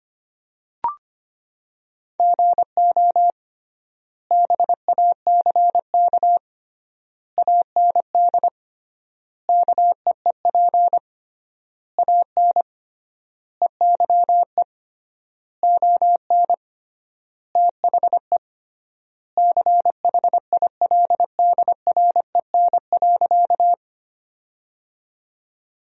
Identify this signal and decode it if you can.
Morse code, heard as a keyed tone, about 25 wpm, 700 Hz. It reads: GO BACK AND KEEP AN EYE ON THE CHILDREN.
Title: 3x Word-Spacing - Sentences from Top 300 Words - 25wpm